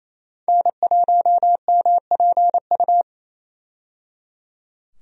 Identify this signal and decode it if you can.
Morse code, heard as a keyed tone, about 27 wpm, 700 Hz. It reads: N1MPU
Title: USA Call Signs - 5 Characters - 28wpm